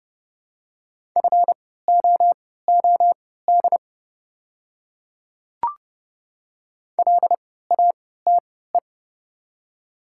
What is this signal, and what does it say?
Morse code, heard as a keyed tone, about 29 wpm, spaced as for 17 wpm, 690 Hz. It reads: FOOD LATE
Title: 3x Character Spacing - Top 500 Words - 30wpm